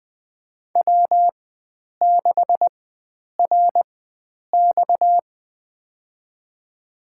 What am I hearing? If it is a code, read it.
Morse code, heard as a keyed tone, about 20 wpm, spaced as for 9 wpm, 700 Hz. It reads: W6RX
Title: USA Call Signs - 4 Characters - 4x Character Spacing - 20wpm